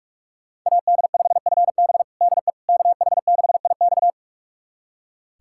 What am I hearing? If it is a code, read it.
Morse code, heard as a keyed tone, about 46 wpm, 700 Hz. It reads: AD5FB DE KH6IX
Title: USA Callsign DE Callsign - 5 Characters - 45wpm